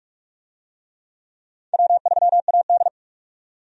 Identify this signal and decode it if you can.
Morse code, heard as a keyed tone, about 43 wpm, 700 Hz. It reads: W3AD
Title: USA Call Signs - 4 Characters - 45wpm